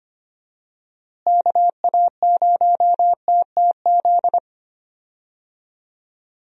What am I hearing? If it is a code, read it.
Morse code, heard as a keyed tone, about 25 wpm, 700 Hz. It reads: KA0TTZ
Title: USA Call Signs - 6 Characters - Repeated 2x - 25wpm